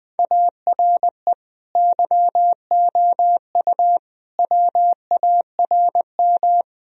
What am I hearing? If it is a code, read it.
Morse code, heard as a keyed tone, about 20 wpm, 700 Hz. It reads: ARE YOU WARM
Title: Sentences from Top 1000 Words - Review Part 2 - 20wpm